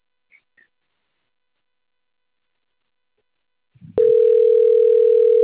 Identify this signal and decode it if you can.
Ringback tone (North America), 440 + 480 Hz.